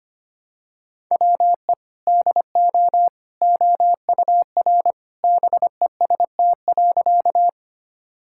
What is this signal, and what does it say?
Morse code, heard as a keyed tone, about 25 wpm, 700 Hz. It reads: WE DO OUR BEST.
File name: practice / 25wpm / Sentences with 2-Word Phrases - 25wpm